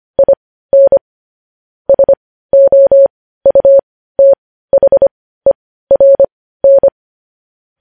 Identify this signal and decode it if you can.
Morse code, heard as a keyed tone, about 25 wpm, spaced as for 15 wpm, 550 Hz. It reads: IN SOUTHERN
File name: podcast / News headlines-1x-25x15